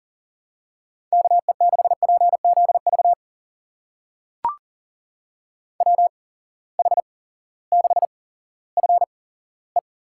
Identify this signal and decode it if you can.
Morse code, heard as a keyed tone, about 39 wpm, spaced as for 18 wpm, 700 Hz. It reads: KE6PZV WH6FE
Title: USA Call Signs - 6 Characters - 8x Character Spacing - 40wpm